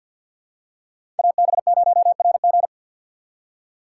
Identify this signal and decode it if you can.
Morse code, heard as a keyed tone, about 50 wpm, 690 Hz. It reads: AB0RG
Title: USA Call Signs - 5 Characters - 2x Character Spacing - 50wpm